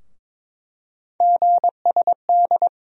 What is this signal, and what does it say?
Morse code, heard as a keyed tone, about 22 wpm, 700 Hz. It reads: GSD
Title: USA Call Sign Postfixes - 3 Letters 22wpm